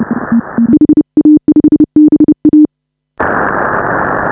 Morse code, heard as a keyed tone, about 29 wpm, 300 Hz. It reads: HA5BA